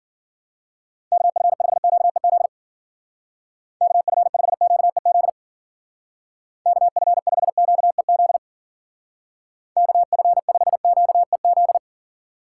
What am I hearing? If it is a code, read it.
Morse code, heard as a keyed tone, about 51 wpm, 700 Hz. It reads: KF5QEZ KF5QEZ KF5QEZ KF5QEZ